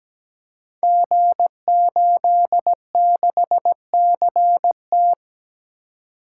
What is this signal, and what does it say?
Morse code, heard as a keyed tone, about 17 wpm, 700 Hz. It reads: G86CT
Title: Code Groups - 5 Letter:Numbers - 17wpm